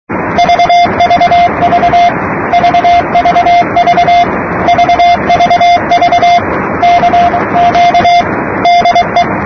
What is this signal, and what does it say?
Morse code, heard as a keyed tone, about 23 wpm, 700 Hz, with radio noise. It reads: VVV VVV VVV CQ DE
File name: namerica / usa / usa-8280-ksm-22445-8-khz-15-03-2008-2125-utc-gschwartz